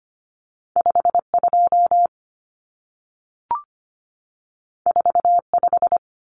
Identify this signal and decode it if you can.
Morse code, heard as a keyed tone, about 25 wpm, 700 Hz. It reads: 52 45
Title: Numbers - 2-Digit 25wpm